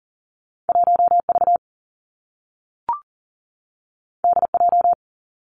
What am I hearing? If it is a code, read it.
Morse code, heard as a keyed tone, about 42 wpm, 700 Hz. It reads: 1 V D J
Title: Two Challenging Letter-Number Combinations 40wpm